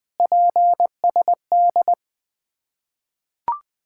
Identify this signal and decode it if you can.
Morse code, heard as a keyed tone, about 20 wpm, 700 Hz. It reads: PSD
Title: Ontario QSO Party Exchange 20wpm